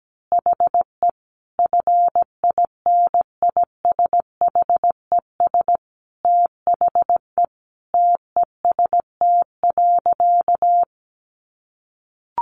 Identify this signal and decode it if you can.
Morse code, heard as a keyed tone, about 17 wpm, 700 Hz. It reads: HE FINISHES THE TEST.
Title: Additional Easy Phrases 17wpm